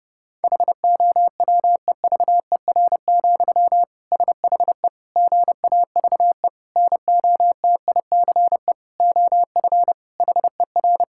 Morse code, heard as a keyed tone, about 32 wpm, 700 Hz. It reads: HOWEVER, SHE GAVE NOTICE OF HER
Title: Sentences from Top 1000 Words - Review Part 6 - 30wpm